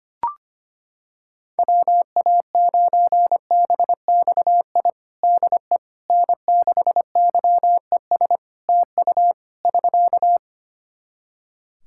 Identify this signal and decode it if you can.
Morse code, heard as a keyed tone, about 25 wpm, 700 Hz. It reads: WA9BXI DE N6YES TU <SK>
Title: Practice QSO - Exchange 4 - Ending - 25wpm